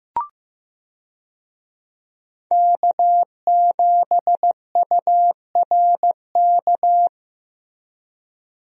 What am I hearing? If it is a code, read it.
Morse code, heard as a keyed tone, about 15 wpm, 700 Hz. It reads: K7URK
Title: USA Call Signs - 5 Characters - Repeated 2x - 15wpm